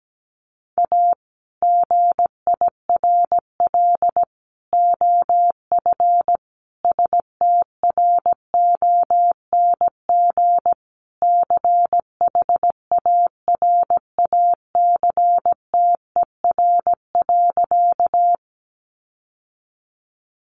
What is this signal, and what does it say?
Morse code, heard as a keyed tone, about 17 wpm, 700 Hz. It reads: A GIRL OF STRONG CHARACTER.
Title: Sentences from Top 1200 Words 17wpm